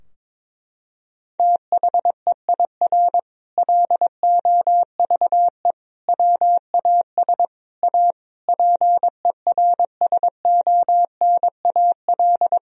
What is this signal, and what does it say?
Morse code, heard as a keyed tone, about 22 wpm, 700 Hz. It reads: THEIR LOVE WAS A PERSONAL
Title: Sentences from Top 1000 Words - Review Part 4 - 22wpm